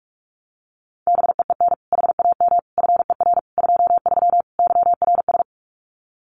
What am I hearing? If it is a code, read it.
Morse code, heard as a keyed tone, about 44 wpm, 700 Hz. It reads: BEEN HAM FER 23 YRS
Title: Common QSO Phrases Encore 45wpm